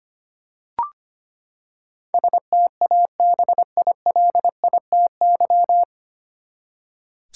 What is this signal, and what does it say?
Morse code, heard as a keyed tone, about 25 wpm, 700 Hz. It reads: STABILITY
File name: Nine Letter Words - 25wpm